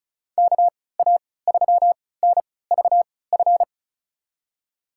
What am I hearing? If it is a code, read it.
Morse code, heard as a keyed tone, about 36 wpm, spaced as for 20 wpm, 700 Hz. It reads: KA3NVF
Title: USA Call Signs - 6 Characters - 3x Character Spacing - 35wpm